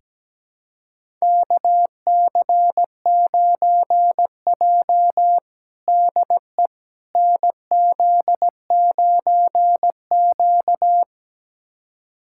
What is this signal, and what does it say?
Morse code, heard as a keyed tone, about 17 wpm, 700 Hz. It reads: KC9J DE NZ9Q
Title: USA Callsign DE Callsign - 4 Characters - 17wpm